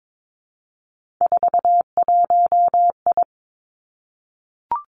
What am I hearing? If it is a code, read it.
Morse code, heard as a keyed tone, about 22 wpm, 700 Hz. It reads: 41I
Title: Three Challenging Letter-Number Combinations 22wpm